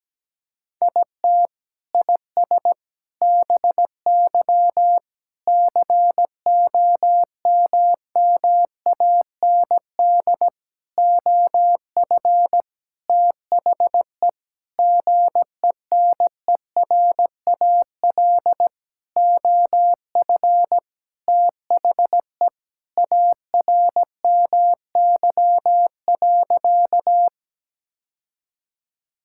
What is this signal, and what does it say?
Morse code, heard as a keyed tone, about 17 wpm, 700 Hz. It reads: IT IS BY COMMAND OF THE GENERAL OF THE ARMY.
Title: Sentences from Top 1600 Words 17wpm